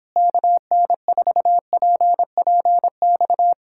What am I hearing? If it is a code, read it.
Morse code, heard as a keyed tone, about 26 wpm, 700 Hz. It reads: KN4PPX
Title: Speed-Racing Common QSO Phrases - 22wpm